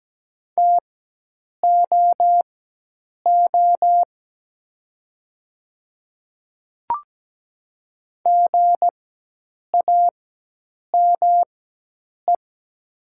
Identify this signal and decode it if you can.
Morse code, heard as a keyed tone, about 17 wpm, spaced as for 8 wpm, 700 Hz. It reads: TOO GAME
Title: 4x Character Spacing - Top 500 Words - 17wpm